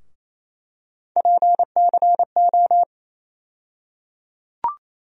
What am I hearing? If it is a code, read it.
Morse code, heard as a keyed tone, about 28 wpm, 700 Hz. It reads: PCO
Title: USA Call Sign Postfixes - 3 Letters 28wpm